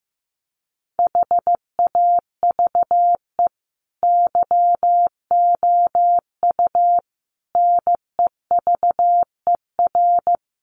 Morse code, heard as a keyed tone, about 15 wpm, 700 Hz. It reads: HAVE YOU NEVER
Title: Sentences from Top 300 Words 15wpm